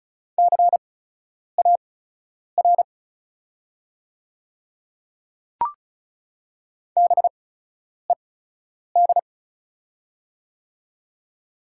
Morse code, heard as a keyed tone, about 36 wpm, spaced as for 10 wpm, 700 Hz. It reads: CAR BED
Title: 8x Character Spacing - Top 500 Words - 35wpm